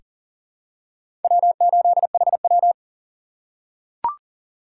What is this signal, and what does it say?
Morse code, heard as a keyed tone, about 37 wpm, 700 Hz. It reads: W8HW
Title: K1USN SST QSO Practice - 40wpm